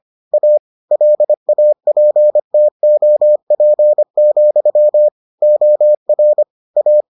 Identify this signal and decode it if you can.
Morse code, heard as a keyed tone, about 25 wpm, 600 Hz. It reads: A LAPTOP, OR A